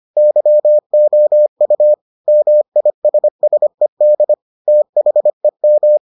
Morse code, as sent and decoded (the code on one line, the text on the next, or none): -.-- --- ..- / -- .. ... ... -.. / - .... --
YOU MISSED THEM